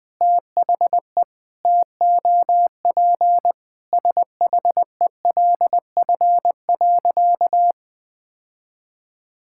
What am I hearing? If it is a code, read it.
Morse code, heard as a keyed tone, about 19 wpm, 700 Hz. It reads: THE TOP SHELF.
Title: Sentences with 6-Word Phrases - Spoken Beginning - 20wpm